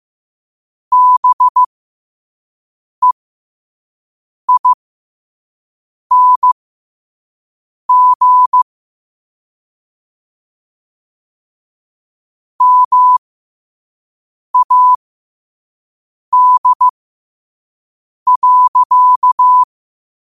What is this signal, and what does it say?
Morse code, heard as a keyed tone, about 15 wpm, spaced as for 5 wpm, 1000 Hz. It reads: BEING MAD.
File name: qotdfiles / 5wpm